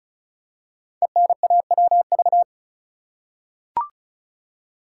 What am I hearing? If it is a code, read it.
Morse code, heard as a keyed tone, about 36 wpm, 690 Hz. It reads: ENAWV
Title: Code Groups - 5 Letters - 35wpm